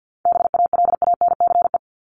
Morse code, heard as a keyed tone, about 50 wpm, 700 Hz. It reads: BALANCE
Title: Top 751-1000 Words 50wpm